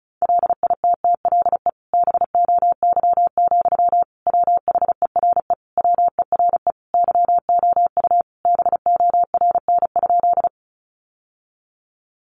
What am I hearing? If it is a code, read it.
Morse code, heard as a keyed tone, about 34 wpm, 700 Hz. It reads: LITTLE BOY, WHERE WERE YOU BORN?
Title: Sentences from Top 1000 Words 35wpm